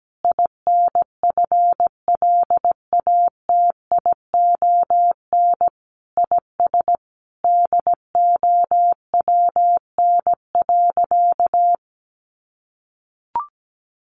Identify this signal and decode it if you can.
Morse code, heard as a keyed tone, about 17 wpm, 700 Hz. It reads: INFLATION IS DOWN.
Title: Sentences from Top 1100 Words 17wpm